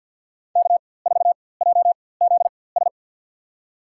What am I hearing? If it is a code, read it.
Morse code, heard as a keyed tone, about 47 wpm, spaced as for 23 wpm, 700 Hz. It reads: K4JZS
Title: USA Call Signs - 5 Characters - 4x Character Spacing - 50wpm